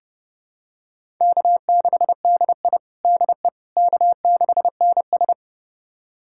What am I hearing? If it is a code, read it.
Morse code, heard as a keyed tone, about 31 wpm, 700 Hz. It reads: K6DI DE K6NS
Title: USA Callsign DE Callsign - 4 Characters - 30wpm